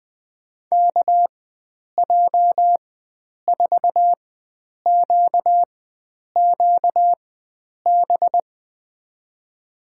Morse code, heard as a keyed tone, about 20 wpm, spaced as for 9 wpm, 700 Hz. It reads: KJ4QQB
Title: USA Call Signs - 4x Character Spacing - 20wpm